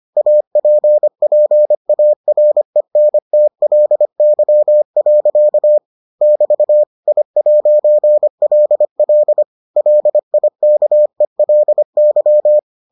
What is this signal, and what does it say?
Morse code, heard as a keyed tone, about 25 wpm, 600 Hz. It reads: APPARENTLY. <BT> I'LL LIKELY